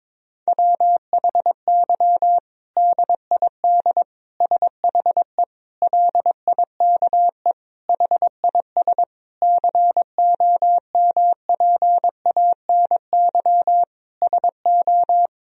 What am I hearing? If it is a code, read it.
Morse code, heard as a keyed tone, about 22 wpm, 700 Hz. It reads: WHY DID SHE LIKE HIS COMPANY SO
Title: Sentences from Top 1000 Words - Review Part 3 - 22wpm